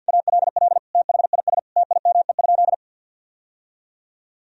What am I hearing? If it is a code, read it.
Morse code, heard as a keyed tone, about 50 wpm, 700 Hz. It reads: ALL THIS TIME?
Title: Sentences from Top 1000 Words - Review Part 2 - 50wpm